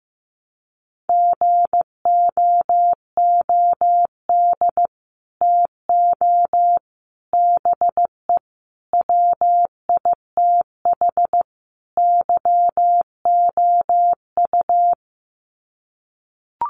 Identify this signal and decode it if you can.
Morse code, heard as a keyed tone, about 15 wpm, 700 Hz. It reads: GOOD TO BE WITH YOU
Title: Sets of 5 Words Encore2 15wpm